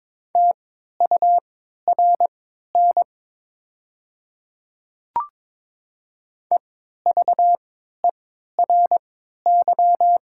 Morse code, heard as a keyed tone, about 22 wpm, spaced as for 12 wpm, 700 Hz. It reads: TURN EVERY
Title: 3x Character Spacing - Top 200 Words - 22wpm